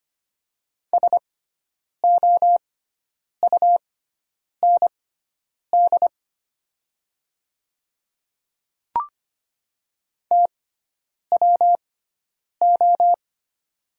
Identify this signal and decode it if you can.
Morse code, heard as a keyed tone, about 25 wpm, spaced as for 9 wpm, 700 Hz. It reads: SOUND TWO
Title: 6x Character Spacing - Top 100 Words - 25wpm